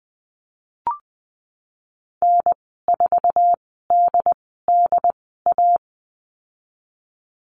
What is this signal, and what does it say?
Morse code, heard as a keyed tone, about 20 wpm, spaced as for 14 wpm, 700 Hz. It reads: N4DDA
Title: 082 Lesson 14g - Number 2 - Call Signs 20wpm